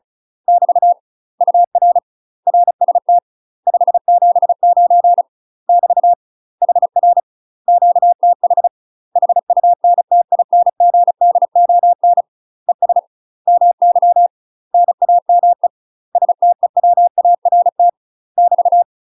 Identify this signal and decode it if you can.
Morse code, heard as a keyed tone, about 36 wpm, 700 Hz. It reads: <BT> UR RST 579 <BT> HR QTH HUNTINGDON ES MY NAME STEWART <BT>
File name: UK MASTER QSOs TRAINING / 35WPM / UK MASTER QSOs 0033